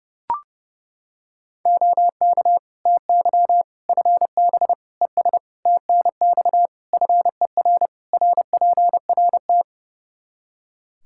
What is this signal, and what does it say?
Morse code, heard as a keyed tone, about 30 wpm, 700 Hz. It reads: OK TY FB ES TNX FER RPRT
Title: Practice QSO - Exchange 2 - Info - 30wpm